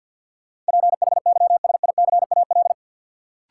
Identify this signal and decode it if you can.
Morse code, heard as a keyed tone, about 53 wpm, 700 Hz. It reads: PHYSICAL
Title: Sentences from Top 1400 Words - New Word Drill Alt - 50wpm